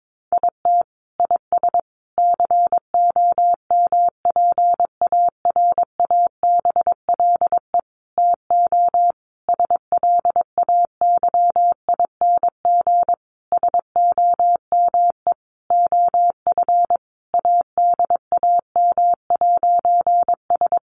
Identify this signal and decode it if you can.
Morse code, heard as a keyed tone, about 22 wpm, 700 Hz. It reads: IT IS COMPARABLE TO SLAYING SOME OF ADAM'S
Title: A Sound Of Thunder 22wpm